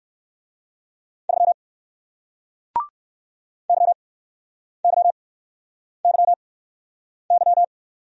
Morse code, heard as a keyed tone, about 60 wpm, spaced as for 9 wpm, 700 Hz. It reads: 3 YYYY